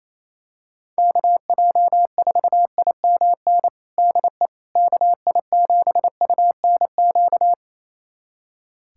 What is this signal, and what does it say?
Morse code, heard as a keyed tone, about 28 wpm, 700 Hz. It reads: KJ4IMN DE KI7UNQ
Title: USA Callsign DE Callsign - 6 Characters - 28wpm